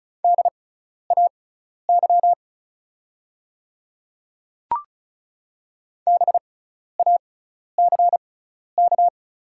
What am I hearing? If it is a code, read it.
Morse code, heard as a keyed tone, about 36 wpm, spaced as for 12 wpm, 700 Hz. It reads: DAY BACK
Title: 6x Character Spacing - Top 200 Words - 35wpm